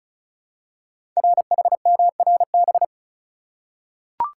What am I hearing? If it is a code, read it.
Morse code, heard as a keyed tone, about 36 wpm, 700 Hz. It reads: RHMRB
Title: Code Groups - 5 Letters - 35wpm